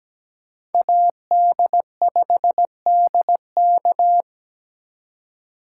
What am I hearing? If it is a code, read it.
Morse code, heard as a keyed tone, about 20 wpm, 700 Hz. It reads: AD5DK
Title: Speed Racing - USA 2x2 Call Signs 17wpm